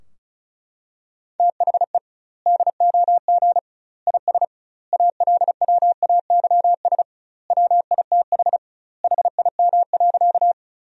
Morse code, heard as a keyed tone, about 34 wpm, 700 Hz. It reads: THE DOG IS ALWAYS WITH HIM.
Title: 2x Word-Spacing - Sentences from Top 400 Words - 35wpm